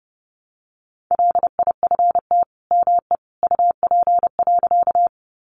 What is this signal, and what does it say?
Morse code, heard as a keyed tone, about 28 wpm, 700 Hz. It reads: LIFT ME UP.
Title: Sentences from Top 1000 Words 30wpm